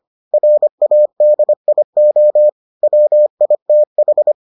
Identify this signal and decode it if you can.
Morse code, heard as a keyed tone, about 25 wpm, 600 Hz. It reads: RADIO WITH